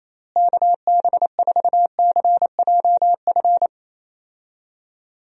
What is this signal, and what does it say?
Morse code, heard as a keyed tone, about 28 wpm, 700 Hz. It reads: KB4CJF
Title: USA Call Signs - 6 Characters - Repeated 2x - 28wpm